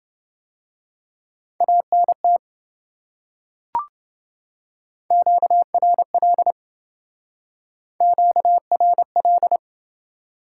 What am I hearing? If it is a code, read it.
Morse code, heard as a keyed tone, about 28 wpm, 700 Hz. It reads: ANT QRL QRL